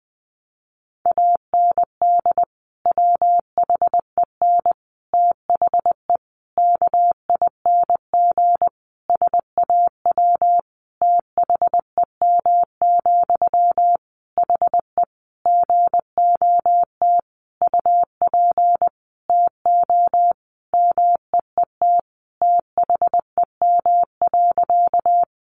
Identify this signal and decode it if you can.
Morse code, heard as a keyed tone, about 20 wpm, 700 Hz. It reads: AND WHEN THE KING SAW THEM, HE GOT UP TO MEET THEM.
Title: Sentences from Top 1100 Words 20wpm